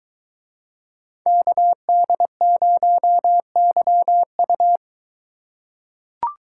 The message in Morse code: KD0YU

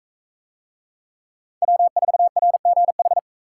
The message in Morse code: W4RGH

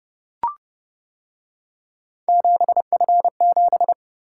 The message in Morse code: 7F7

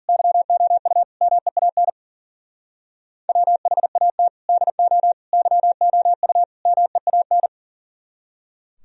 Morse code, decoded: YOU MEAN WHAT DO YOU MEAN

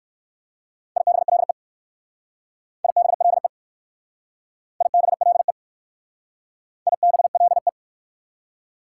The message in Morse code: IBLE IBLE IBLE IBLE

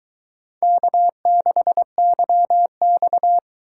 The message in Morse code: K6YX

